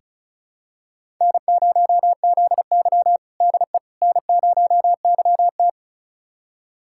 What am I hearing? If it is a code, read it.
Morse code, heard as a keyed tone, about 36 wpm, 700 Hz. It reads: N0ZY DE N0YT
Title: USA Callsign DE Callsign - 4 Characters - 35wpm